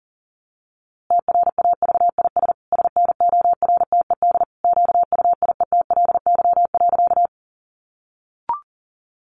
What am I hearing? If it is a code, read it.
Morse code, heard as a keyed tone, about 40 wpm, 700 Hz. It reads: TRAVIS SNORTED QUIETLY.
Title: A Sound Of Thunder 40wpm